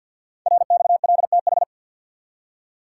Morse code, heard as a keyed tone, about 48 wpm, 700 Hz. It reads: RXLTH
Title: Code Groups - 5 Letters - 50wpm